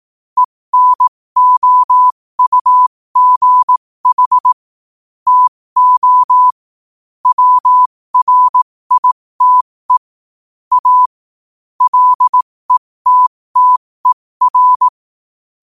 Morse code, heard as a keyed tone, about 18 wpm, spaced as for 15 wpm, 1000 Hz. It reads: ENOUGH TO WRITE A LETTER
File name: qotdfiles / 15wpm